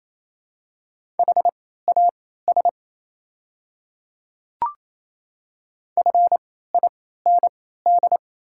Morse code, heard as a keyed tone, about 27 wpm, spaced as for 16 wpm, 690 Hz. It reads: HAS FIND
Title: 3x Character Spacing - Top 100 Words - 28wpm